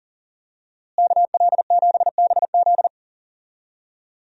Morse code, heard as a keyed tone, about 40 wpm, 700 Hz. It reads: KL7BZ
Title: DXCC-100 Call Signs - 5 Characters - Repeated 2x - 40wpm